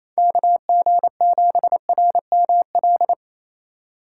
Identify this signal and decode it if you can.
Morse code, heard as a keyed tone, about 34 wpm, 700 Hz. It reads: KG7RML
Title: Speed Racing - USA 2x3 Call Signs 28wpm